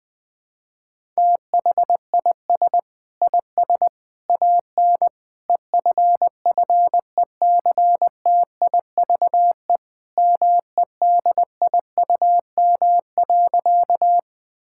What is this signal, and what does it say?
Morse code, heard as a keyed tone, about 20 wpm, 700 Hz. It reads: THIS IS AN EFFECTIVE MEDIUM.
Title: Sentences from Top 1000 Words - Review Part 6 - 20wpm